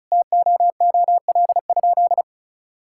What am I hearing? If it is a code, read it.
Morse code, heard as a keyed tone, about 36 wpm, 700 Hz. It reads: TOOL?
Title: Sentences from Top 1000 Words - Review Part 6 - 35wpm